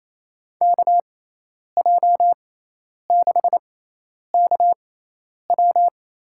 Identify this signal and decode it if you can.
Morse code, heard as a keyed tone, about 28 wpm, spaced as for 10 wpm, 700 Hz. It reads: KJ6KW
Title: USA Call Signs - 6 Characters - 6x Character Spacing - 28wpm